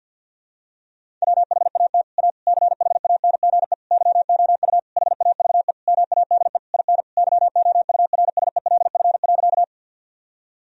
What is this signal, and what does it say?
Morse code, heard as a keyed tone, about 52 wpm, 700 Hz. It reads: WHAT A CHANGE YOU HAVE MADE IN YOURSELF.
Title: Sentences from Top 1000 Words - Review Part 1 - 50wpm